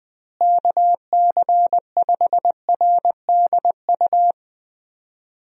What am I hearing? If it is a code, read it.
Morse code, heard as a keyed tone, about 21 wpm, 700 Hz. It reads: KC5RDU
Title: USA Call Signs - 6 Characters - Repeated 3x - 20wpm